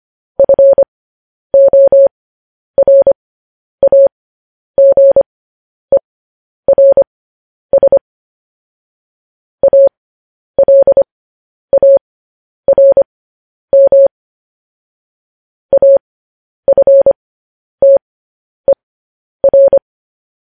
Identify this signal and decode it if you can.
Morse code, heard as a keyed tone, about 25 wpm, spaced as for 10 wpm, 550 Hz. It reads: FORAGERS ALARM AFTER
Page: news-headlines-in-morse-code-at-10-wpm